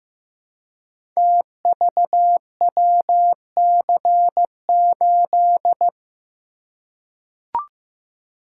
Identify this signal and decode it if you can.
Morse code, heard as a keyed tone, about 15 wpm, 700 Hz. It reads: TVWC8
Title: Code Groups - 5 Letter:Numbers - 15wpm